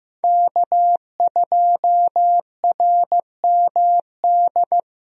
Morse code, heard as a keyed tone, about 15 wpm, 700 Hz. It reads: K2RMD
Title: USA Call Signs - 5 Characters - Repeated 2x - 15wpm